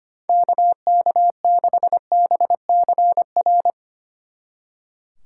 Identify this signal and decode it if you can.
Morse code, heard as a keyed tone, about 31 wpm, 700 Hz. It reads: KK6BCR